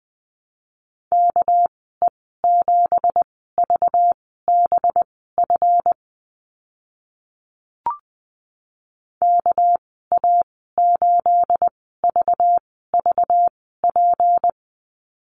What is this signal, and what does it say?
Morse code, heard as a keyed tone, about 20 wpm, spaced as for 14 wpm, 700 Hz. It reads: KE7VBF KA8VVP
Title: 213 Lesson 34g - Number 8 - Call Signs 20wpm